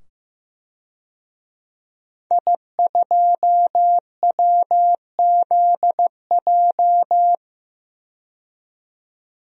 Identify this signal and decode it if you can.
Morse code, heard as a keyed tone, about 15 wpm, 700 Hz. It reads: I2WZJ
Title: DXCC-100 Call Signs - 5 Characters - 15wpm